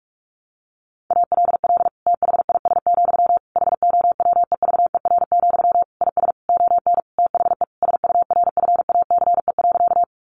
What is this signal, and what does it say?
Morse code, heard as a keyed tone, about 44 wpm, 700 Hz. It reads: ALL THIS, HOWEVER, IS ON THE SURFACE.
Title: Sentences from Top 600 Words 45wpm